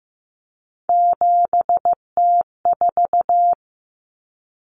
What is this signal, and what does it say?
Morse code, heard as a keyed tone, about 15 wpm, 700 Hz. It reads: 7T4
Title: Common Cut Numbers - 3-Digit 15wpm